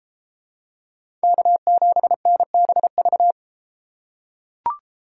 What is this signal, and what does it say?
Morse code, heard as a keyed tone, about 31 wpm, 700 Hz. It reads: K7NBV